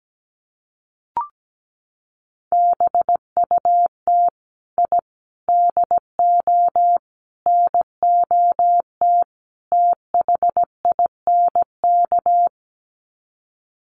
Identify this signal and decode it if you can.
Morse code, heard as a keyed tone, about 17 wpm, 700 Hz. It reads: BUT I DO NOT THINK
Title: Sets of 5 Words 17wpm